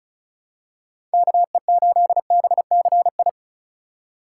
Morse code, heard as a keyed tone, about 36 wpm, 700 Hz. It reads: KE8BCI
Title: USA Call Signs - 6 Characters - 3x Character Spacing - 35wpm